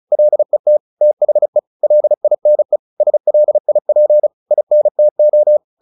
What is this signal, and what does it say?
Morse code, heard as a keyed tone, about 34 wpm, 600 Hz. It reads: LET THE LINE SLIP INTO
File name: the old man and the sea 2 35wpm